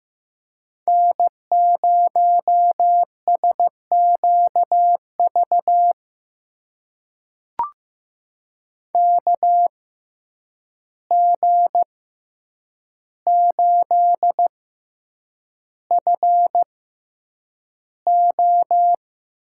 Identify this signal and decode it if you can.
Morse code, heard as a keyed tone, about 15 wpm, spaced as for 8 wpm, 700 Hz. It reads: N0SQV KG8FO
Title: USA Call Signs - 5 Characters - 6x Character Spacing - 15wpm